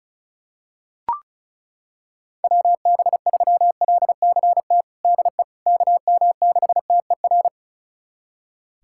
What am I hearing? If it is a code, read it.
Morse code, heard as a keyed tone, about 36 wpm, 700 Hz. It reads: WB3LCT DE KM6TER
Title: USA Callsign DE Callsign 35wpm